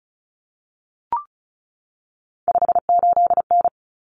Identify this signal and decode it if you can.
Morse code, heard as a keyed tone, about 34 wpm, 700 Hz. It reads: 58N
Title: Common Cut Numbers - 3-Digit 35wpm